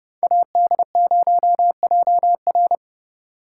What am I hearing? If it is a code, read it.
Morse code, heard as a keyed tone, about 30 wpm, 700 Hz. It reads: AD0JR